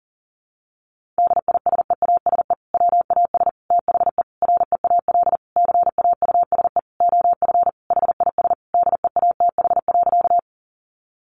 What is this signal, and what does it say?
Morse code, heard as a keyed tone, about 40 wpm, 700 Hz. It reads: DISEASE WAS THE REAL CAUSE OF HIS DEATH.
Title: Sentences from Top 1100 Words 40wpm